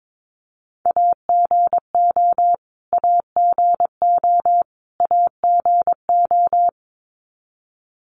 Morse code, heard as a keyed tone, about 22 wpm, 700 Hz. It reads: AGO AGO AGO
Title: Top 500 Words Repeated 22wpm